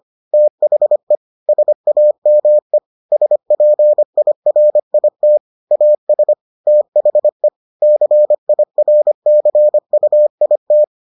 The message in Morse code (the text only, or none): THE SAME SPIRIT AS THE CIRCUIT